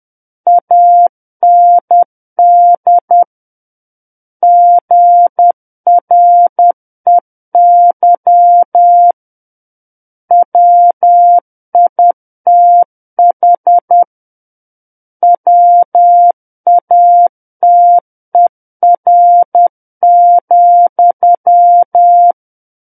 Morse code, heard as a keyed tone, about 10 wpm, 700 Hz. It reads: AND GREY WITH WATER,
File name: Book One / 01-Chapter 1 - 10 WPM